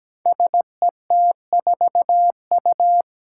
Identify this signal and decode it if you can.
Morse code, heard as a keyed tone, about 17 wpm, 700 Hz. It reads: SET4U
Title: Five Challenging Letter-Number Combinations 17wpm